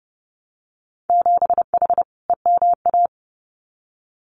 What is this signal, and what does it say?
Morse code, heard as a keyed tone, about 30 wpm, 700 Hz. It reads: 7H EMA
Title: Winter Field Day Exchanges Encore 30wpm